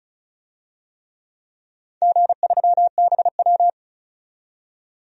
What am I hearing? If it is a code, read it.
Morse code, heard as a keyed tone, about 36 wpm, 700 Hz. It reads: G3BW
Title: DXCC-100 Call Signs - 4 Characters - 35wpm